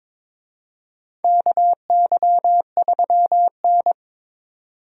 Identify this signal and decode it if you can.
Morse code, heard as a keyed tone, about 22 wpm, 700 Hz. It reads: KY3N
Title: USA Call Signs - 4 Characters - 22wpm